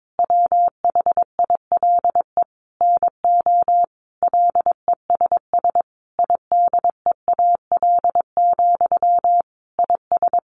Morse code, heard as a keyed tone, about 22 wpm, 700 Hz. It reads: WHILE NO LESS IDEAL, IS
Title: Sentences from Top 1400 Words 22wpm